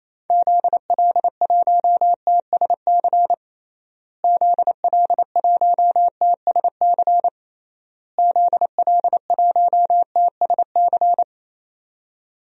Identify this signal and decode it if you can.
Morse code, heard as a keyed tone, about 27 wpm, 700 Hz. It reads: ZL1TSC ZL1TSC ZL1TSC
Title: DXCC-100 Call Signs - 6 Characters - Repeated 3x - 28wpm